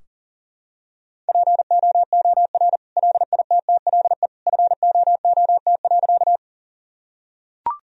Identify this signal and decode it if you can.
Morse code, heard as a keyed tone, about 40 wpm, 700 Hz. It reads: POOR LITTLE FOOT.
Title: Sentences from Top 1000 Words - Review Part 3 - 40wpm